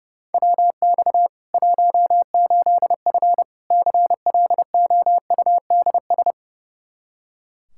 Morse code, heard as a keyed tone, about 32 wpm, 700 Hz. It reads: WX 18F CLOUDS
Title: Practice QSO - Exchange 2 - Info - 30wpm